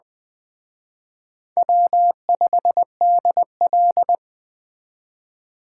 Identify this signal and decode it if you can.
Morse code, heard as a keyed tone, about 20 wpm, 700 Hz. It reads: W5DL